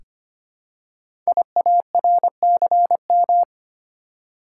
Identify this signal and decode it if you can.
Morse code, heard as a keyed tone, about 25 wpm, 690 Hz. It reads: IARCM
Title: Code Groups - 5 Letters - 25wpm